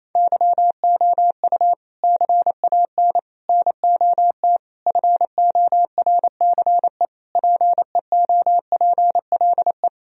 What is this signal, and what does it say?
Morse code, heard as a keyed tone, about 27 wpm, 700 Hz. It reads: YOU CAN NOT FORCE PEOPLE
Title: Sentences from Top 1000 Words - Review Part 3 - 28wpm